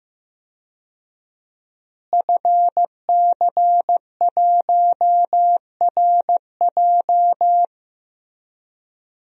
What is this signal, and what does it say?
Morse code, heard as a keyed tone, about 15 wpm, 700 Hz. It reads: FC1RJ
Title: DXCC-100 Call Signs - 5 Characters - Repeated 2x - 15wpm